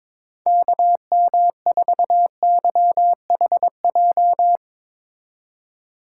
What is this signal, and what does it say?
Morse code, heard as a keyed tone, about 22 wpm, 700 Hz. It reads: KM4YHJ
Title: USA Call Signs - 6 Characters - 3x Character Spacing - 22wpm